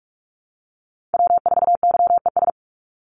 Morse code, heard as a keyed tone, about 44 wpm, 700 Hz. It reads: W4YES
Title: Call Signs Repeated 3x - 45wpm